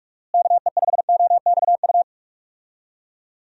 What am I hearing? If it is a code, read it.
Morse code, heard as a keyed tone, about 46 wpm, 700 Hz. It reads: KE5OXU
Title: USA Call Signs - 6 Characters - Repeated 3x - 45wpm